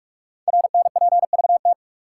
Morse code, heard as a keyed tone, about 44 wpm, 700 Hz. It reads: RNPVT
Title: Code Groups - 5 Letters - 45wpm